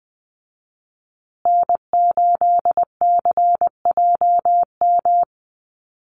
Morse code, heard as a keyed tone, about 20 wpm, 700 Hz. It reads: N8CJM
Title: USA Call Signs Repeated 3x 20wpm